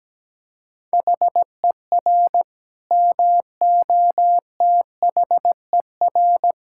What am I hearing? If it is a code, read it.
Morse code, heard as a keyed tone, about 17 wpm, 700 Hz. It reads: HER MOTHER
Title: Sentences from Top 1000 Words - Review Part 5 - 17wpm